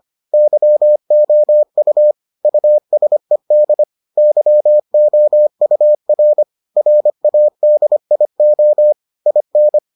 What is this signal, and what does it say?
Morse code, heard as a keyed tone, about 25 wpm, 600 Hz. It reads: YOU USED YOUR RADIO IN